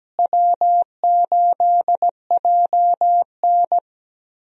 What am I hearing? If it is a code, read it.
Morse code, heard as a keyed tone, about 17 wpm, 700 Hz. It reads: W8JN
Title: USA Call Signs - 4 Characters - Repeated 2x - 17wpm